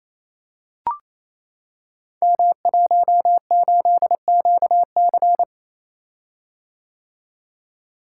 Morse code, heard as a keyed tone, about 28 wpm, 700 Hz. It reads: M18QC
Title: Code Groups - 5 Letter:Numbers - 28wpm